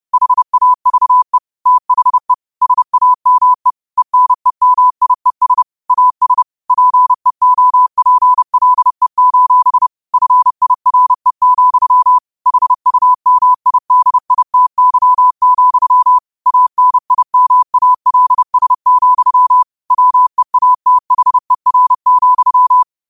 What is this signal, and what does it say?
Morse code, heard as a keyed tone, about 30 wpm, 1000 Hz. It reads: HAVE THE SAME ENEMIES AS PEOPLE: FIRE, HUMIDITY, ANIMALS, WEATHER,
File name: qotdfiles / 30wpm